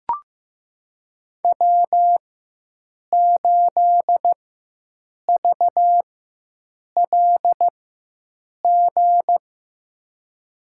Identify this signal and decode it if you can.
Morse code, heard as a keyed tone, about 15 wpm, spaced as for 7 wpm, 700 Hz. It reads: W8VLG
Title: USA Call Signs - 5 Characters - 4x Character Spacing - 15wpm